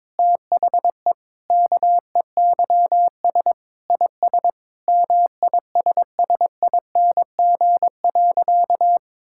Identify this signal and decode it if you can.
Morse code, heard as a keyed tone, about 22 wpm, 700 Hz. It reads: THE KEYS IS MISSING.
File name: practice / 22wpm / Sentences with 4-Word Phrases - 22wpm